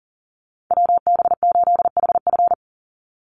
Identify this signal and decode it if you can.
Morse code, heard as a keyed tone, about 40 wpm, 700 Hz. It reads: WB8HF
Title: Call Signs Repeated 3x - 40wpm